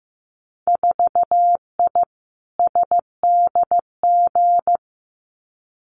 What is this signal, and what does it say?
Morse code, heard as a keyed tone, about 15 wpm, 700 Hz. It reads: 4I SDG
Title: Winter Field Day Exchanges Encore 15wpm